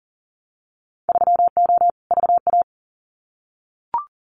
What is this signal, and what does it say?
Morse code, heard as a keyed tone, about 39 wpm, 700 Hz. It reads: 3O VA
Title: Winter Field Day Exchanges 40wpm